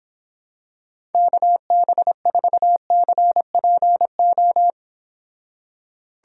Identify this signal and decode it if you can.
Morse code, heard as a keyed tone, about 26 wpm, 700 Hz. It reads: KB4CPO